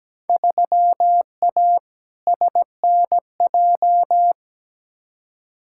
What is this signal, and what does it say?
Morse code, heard as a keyed tone, about 17 wpm, 700 Hz. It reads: 3A SNJ
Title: Field Day Exchanges - 17wpm